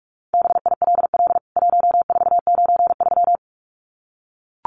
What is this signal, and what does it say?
Morse code, heard as a keyed tone, about 44 wpm, 700 Hz. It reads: BILL 1493
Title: CWT Exchange 45wpm